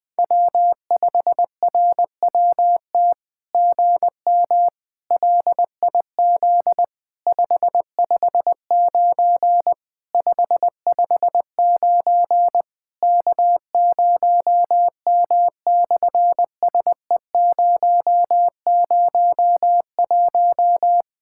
Morse code, heard as a keyed tone, about 20 wpm, 700 Hz. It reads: W5RWT GM LIZ 559 559 K0M/SE001
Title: SOTA QSO Practice 20wpm